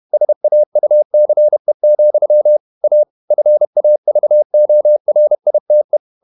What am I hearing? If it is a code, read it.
Morse code, heard as a keyed tone, about 31 wpm, 600 Hz. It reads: SAUCE, A FAVORITE
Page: Never 31wpm | CW med Gnister